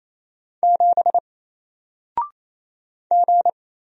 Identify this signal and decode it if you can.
Morse code, heard as a keyed tone, about 27 wpm, 700 Hz. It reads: 7 G
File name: Single-Letter-Number-28wpm